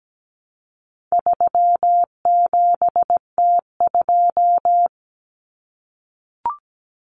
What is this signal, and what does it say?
Morse code, heard as a keyed tone, about 17 wpm, 700 Hz. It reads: 37T2
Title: Common Cut Numbers - 4-Digit 17wpm